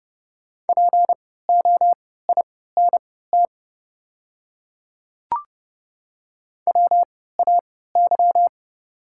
Morse code, heard as a keyed tone, about 30 wpm, spaced as for 17 wpm, 690 Hz. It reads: POINT WAY
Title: 3x Character Spacing - Top 300 Words - 30wpm